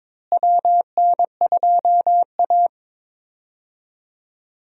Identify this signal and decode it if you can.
Morse code, heard as a keyed tone, about 22 wpm, 700 Hz. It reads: WN2A
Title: USA Call Signs - 4 Characters - Repeated 2x - 22wpm